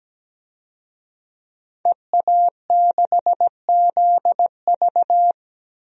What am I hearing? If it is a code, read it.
Morse code, heard as a keyed tone, about 17 wpm, 700 Hz. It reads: EA6ZV
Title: DXCC-100 Call Signs - 5 Characters - 17wpm